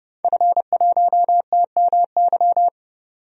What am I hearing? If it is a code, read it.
Morse code, heard as a keyed tone, about 30 wpm, 700 Hz. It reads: F1TMY
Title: DXCC-100 Call Signs - 5 Characters - 30wpm